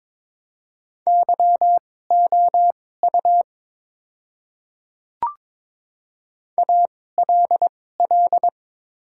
Morse code, heard as a keyed tone, about 22 wpm, spaced as for 16 wpm, 700 Hz. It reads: YOU ALL
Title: 2x Character Spacing - Top 300 Words - 22wpm